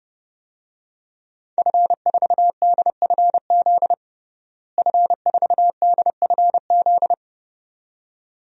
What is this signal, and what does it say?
Morse code, heard as a keyed tone, about 30 wpm, 700 Hz. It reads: F4DFZ F4DFZ
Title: DXCC-100 Call Signs - 5 Characters - Repeated 2x - 30wpm